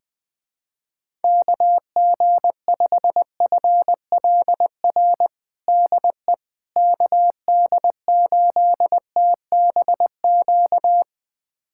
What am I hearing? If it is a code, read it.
Morse code, heard as a keyed tone, about 20 wpm, 700 Hz. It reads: KG5FLR DE KD8TBQ
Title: USA Callsign DE Callsign 20wpm